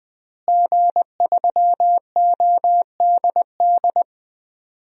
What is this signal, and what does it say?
Morse code, heard as a keyed tone, about 20 wpm, 700 Hz. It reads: G3ODD
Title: DXCC-100 Call Signs - 5 Characters - Repeated 3x - 20wpm